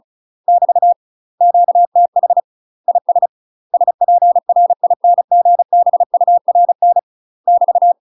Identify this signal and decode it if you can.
Morse code, heard as a keyed tone, about 34 wpm, 700 Hz. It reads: <BT> QTH IS SPRINGBURN <BT>
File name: UK MASTER QSOs TRAINING / 35WPM / UK MASTER QSOs 0024